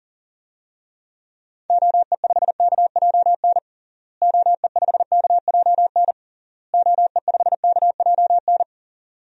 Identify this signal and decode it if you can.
Morse code, heard as a keyed tone, about 40 wpm, 700 Hz. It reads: OE5KJN OE5KJN OE5KJN
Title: DXCC-100 Call Signs - 6 Characters - Repeated 3x - 40wpm